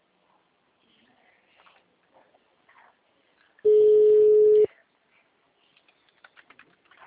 Ringback tone (Europe, 425 Hz).